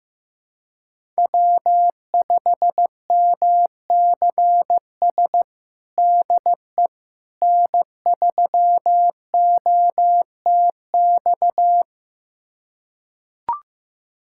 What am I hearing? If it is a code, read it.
Morse code, heard as a keyed tone, about 15 wpm, 700 Hz. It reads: W5MCS DE N3OTX
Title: USA Callsign DE Callsign - 5 Characters - 15wpm